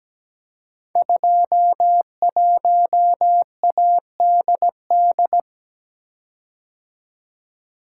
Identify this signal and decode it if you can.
Morse code, heard as a keyed tone, about 17 wpm, 700 Hz. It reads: 21ADD
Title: Code Groups - 5 Letter:Numbers - 17wpm